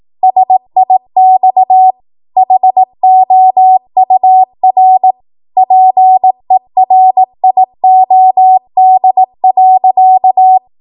Morse code, heard as a keyed tone, about 18 wpm, 750 Hz. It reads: SIX HOUR PERIOD.
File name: Archive / 18 WPM / 18WPM